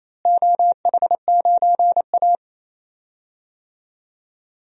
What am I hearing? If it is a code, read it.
Morse code, heard as a keyed tone, about 27 wpm, 700 Hz. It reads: OH9A
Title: DXCC-100 Call Signs - 4 Characters - Repeated 2x - 28wpm